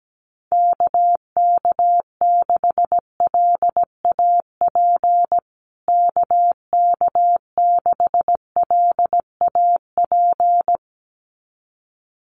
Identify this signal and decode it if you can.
Morse code, heard as a keyed tone, about 17 wpm, 700 Hz. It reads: KK6LAP KK6LAP
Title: USA Call Signs Repeated 2x 17wpm